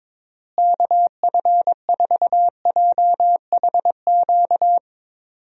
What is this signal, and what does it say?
Morse code, heard as a keyed tone, about 22 wpm, 700 Hz. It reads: KF4JHQ
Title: USA Call Signs - 6 Characters - Repeated 3x - 22wpm